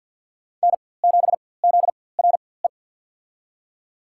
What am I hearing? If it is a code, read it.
Morse code, heard as a keyed tone, about 51 wpm, spaced as for 23 wpm, 690 Hz. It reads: N7ZRE
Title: USA Call Signs - 5 Characters - 4x Character Spacing - 50wpm